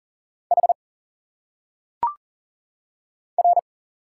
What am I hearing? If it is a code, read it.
Morse code, heard as a keyed tone, about 40 wpm, 700 Hz. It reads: H R
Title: Single Letters - Phonetic Pronunciation 40wpm